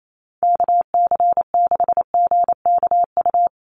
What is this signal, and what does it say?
Morse code, heard as a keyed tone, about 28 wpm, 700 Hz. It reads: KC6GKU
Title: USA Call Signs Repeated 3x 28wpm